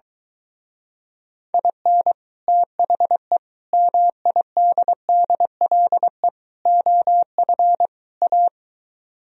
Morse code, heard as a keyed tone, about 22 wpm, 700 Hz. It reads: IN THE MIDDLE OF A